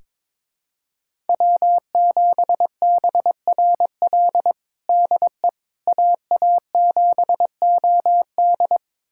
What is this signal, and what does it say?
Morse code, heard as a keyed tone, about 22 wpm, 700 Hz. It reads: W7BRL DE AA7OD
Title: USA Callsign DE Callsign - 5 Characters - 22wpm